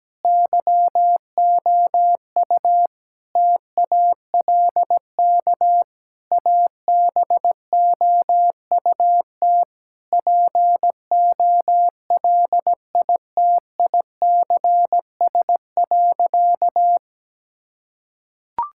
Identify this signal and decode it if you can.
Morse code, heard as a keyed tone, about 17 wpm, 700 Hz. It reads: YOU TALK ABOUT POLITICS.
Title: Sentences from Top 1000 Words - Review Part 6 - 17wpm